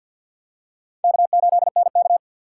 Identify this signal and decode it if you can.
Morse code, heard as a keyed tone, about 49 wpm, 700 Hz. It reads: K8NK
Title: USA Call Signs - 4x Character Spacing - 50wpm